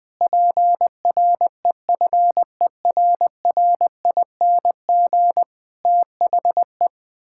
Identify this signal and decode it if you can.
Morse code, heard as a keyed tone, about 20 wpm, 700 Hz. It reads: PREFERRING THE
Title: Pangram Challenge v2 - 20wpm